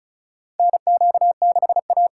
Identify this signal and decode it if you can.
Morse code, heard as a keyed tone, about 36 wpm, 700 Hz. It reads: NQ6A
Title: USA Callsign DE Callsign - 4 Characters - 35wpm